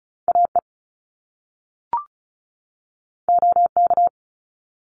Morse code, heard as a keyed tone, about 36 wpm, 700 Hz. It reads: AE OK
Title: Top Two Letter Combinations in Words 35wpm